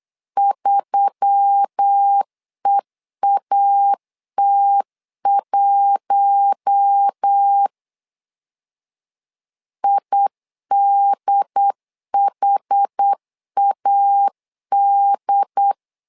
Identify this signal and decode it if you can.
Morse code, heard as a keyed tone, about 8 wpm, 800 Hz. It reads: 3EAT1 IDHAD
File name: LEZIONI / 5 LBVC / 5 LB 8wpm 2816